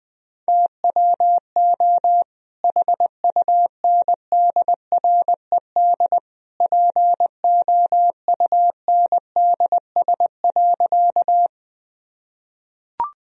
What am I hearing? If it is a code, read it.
Morse code, heard as a keyed tone, about 20 wpm, 700 Hz. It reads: TWO HUNDRED POUNDS.
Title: Sentences from Top 1000 Words - Review Part 2 - 20wpm